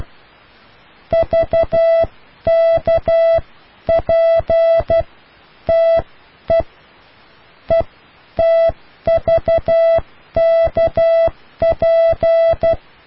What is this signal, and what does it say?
Morse code, heard as a keyed tone, about 12 wpm, spaced as for 10 wpm, 660 Hz. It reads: VKPTE ETVKP